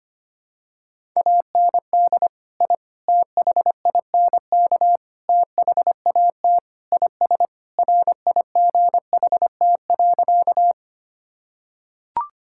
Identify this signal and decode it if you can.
Morse code, heard as a keyed tone, about 25 wpm, 700 Hz. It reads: AND I THINK THAT IS RIGHT.
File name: practice / 25wpm / Sentences with 4-Word Phrases - 25wpm